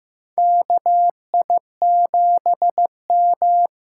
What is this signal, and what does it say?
Morse code, heard as a keyed tone, about 15 wpm, 700 Hz. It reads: KI7M
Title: USA Call Signs - 6 Characters - 8x Character Spacing - 15wpm